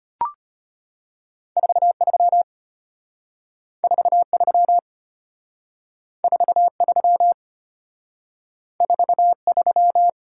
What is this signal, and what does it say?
Morse code, heard as a keyed tone, about 32 wpm, 700 Hz. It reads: 43 43 43 43